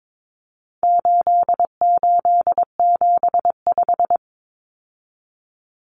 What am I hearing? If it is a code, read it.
Morse code, heard as a keyed tone, about 22 wpm, 700 Hz. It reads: 8875
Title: Common Cut Numbers - 4-Digit 22wpm